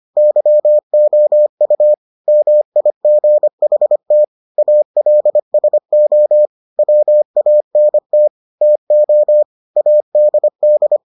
Morse code, heard as a keyed tone, about 25 wpm, 600 Hz. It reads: YOU MIGHT ALSO WANT TO ADD